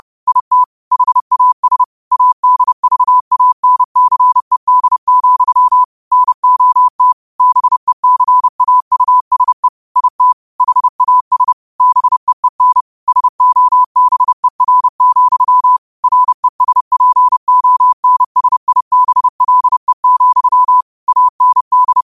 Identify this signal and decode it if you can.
Morse code, heard as a keyed tone, about 30 wpm, 1000 Hz. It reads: IT HAS ADVANCED, NOT BECAUSE IT HAS BEEN SOBER, RESPONSIBLE, AND
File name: qotdfiles / 30wpm